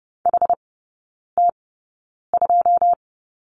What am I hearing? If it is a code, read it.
Morse code, heard as a keyed tone, about 30 wpm, spaced as for 9 wpm, 700 Hz. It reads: HT2
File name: Lesson 18e - Letter m - Three Characters 30wpm